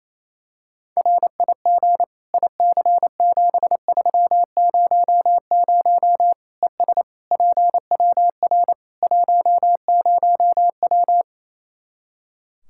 Morse code, heard as a keyed tone, about 28 wpm, 700 Hz. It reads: RIG IC7300 ES PWR 10W
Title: K1USN SST QSO Practice 28wpm